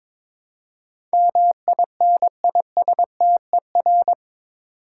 Morse code, heard as a keyed tone, about 21 wpm, 700 Hz. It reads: MINISTER
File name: practice / 22wpm / Eight Letter Words - 22wpm